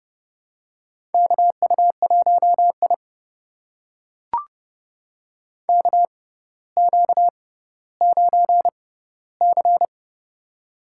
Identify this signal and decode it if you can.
Morse code, heard as a keyed tone, about 30 wpm, spaced as for 15 wpm, 700 Hz. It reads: KU1I KQ9C
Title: USA Call Signs - 4 Characters - 6x Character Spacing - 30wpm